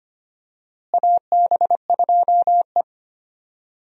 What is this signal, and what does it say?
Morse code, heard as a keyed tone, about 25 wpm, 700 Hz. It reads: AB2E
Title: USA Call Signs - 4 Characters - Repeated 2x - 25wpm